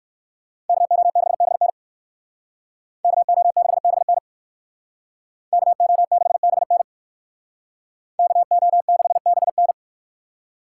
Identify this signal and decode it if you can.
Morse code, heard as a keyed tone, about 59 wpm, 700 Hz. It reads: KO6BN KO6BN KO6BN KO6BN